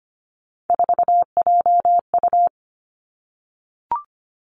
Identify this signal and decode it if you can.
Morse code, heard as a keyed tone, about 25 wpm, 700 Hz. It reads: 4JU
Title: Three Challenging Letter-Number Combinations 25wpm